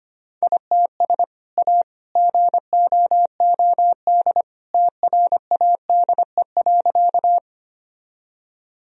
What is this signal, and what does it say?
Morse code, heard as a keyed tone, about 25 wpm, 700 Hz. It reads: ITS A GOOD TRADE.
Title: Sentences from Top 1000 Words - Review Part 4 - 25wpm